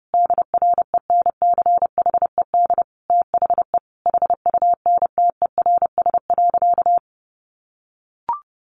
Morse code, heard as a keyed tone, about 30 wpm, 700 Hz. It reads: DRENCHED THE HUNTERS.
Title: A Sound Of Thunder 30wpm